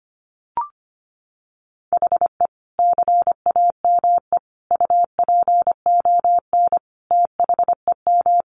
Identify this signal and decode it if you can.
Morse code, heard as a keyed tone, about 25 wpm, 700 Hz. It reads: HE CAME UPON THEM
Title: Sentences from Top 1100 Words 25wpm